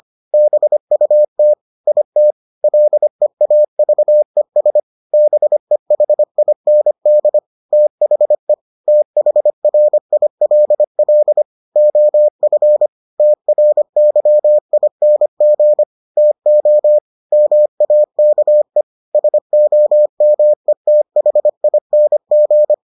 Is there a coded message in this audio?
Yes: Morse code from 0.3 to 22.7 s, heard as a keyed tone, about 25 wpm, 600 Hz. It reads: BUT IT LEAVES BEHIND THE THRILL OF TRYING TO MAKE SOMETHING